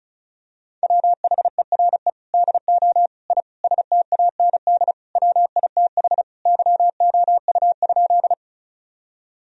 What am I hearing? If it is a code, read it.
Morse code, heard as a keyed tone, about 36 wpm, 700 Hz. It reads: WHERE DO I STAND WITH YOU?
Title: Sentences from Top 1000 Words - Review Part 1 - 35wpm